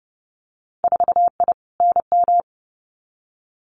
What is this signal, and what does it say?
Morse code, heard as a keyed tone, about 30 wpm, 700 Hz. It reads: 4I NM
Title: Winter Field Day Exchanges Encore 30wpm